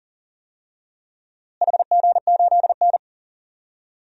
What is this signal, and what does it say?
Morse code, heard as a keyed tone, about 37 wpm, 700 Hz. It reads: HG8N
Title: DXCC-100 Call Signs - 4 Characters - Repeated 2x - 40wpm